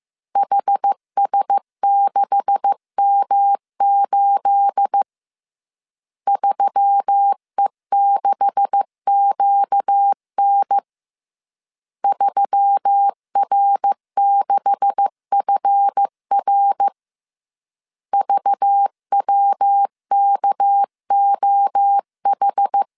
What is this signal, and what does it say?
Morse code, heard as a keyed tone, about 15 wpm, 800 Hz. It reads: HS6M8 3E6QN 3R6FR VWKOH
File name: CW / DET / 1 9 1500C S C 14wpm 3245